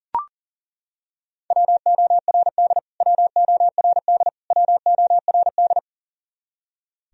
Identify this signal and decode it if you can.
Morse code, heard as a keyed tone, about 40 wpm, 700 Hz. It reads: WORD WORD WORD
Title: Top 100 Words Repeated 40wpm